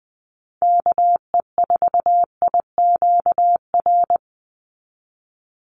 Morse code, heard as a keyed tone, about 20 wpm, 700 Hz. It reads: KE4IQR
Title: USA Call Signs 20wpm